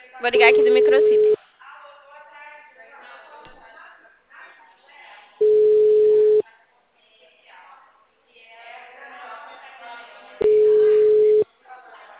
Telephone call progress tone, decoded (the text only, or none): ringback tone